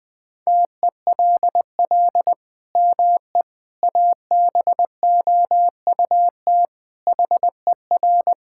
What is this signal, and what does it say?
Morse code, heard as a keyed tone, about 20 wpm, 700 Hz. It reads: TELL ME ABOUT HER
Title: Sentences from Top 1000 Words - Review Part 5 - 20wpm